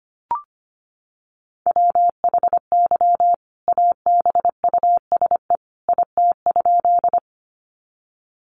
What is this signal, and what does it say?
Morse code, heard as a keyed tone, about 25 wpm, 700 Hz. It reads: WHY ABUSE IT?
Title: Sentences from Top 1300 Words 25wpm